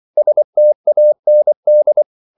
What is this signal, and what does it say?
Morse code, heard as a keyed tone, about 24 wpm, 600 Hz. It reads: STAND